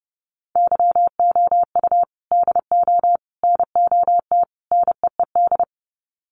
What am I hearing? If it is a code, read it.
Morse code, heard as a keyed tone, about 30 wpm, 700 Hz. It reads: YOU DO NOT NEED
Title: Sets of 4 Words 30wpm